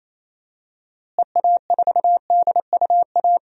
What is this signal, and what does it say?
Morse code, heard as a keyed tone, about 28 wpm, 700 Hz. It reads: EA4DUA
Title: DXCC-100 Call Signs - 6 Characters - Repeated 3x - 28wpm